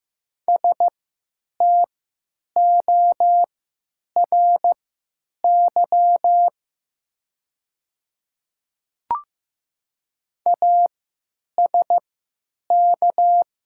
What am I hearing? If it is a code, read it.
Morse code, heard as a keyed tone, about 15 wpm, spaced as for 9 wpm, 700 Hz. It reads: STORY ASK
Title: 3x Character Spacing - Top 300 Words - 15wpm